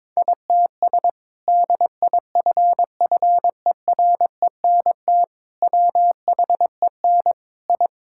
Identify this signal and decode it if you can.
Morse code, heard as a keyed tone, about 22 wpm, 700 Hz. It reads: ITS DIFFERENT WHEN I
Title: Sentences from Top 1000 Words - Review Part 5 - 22wpm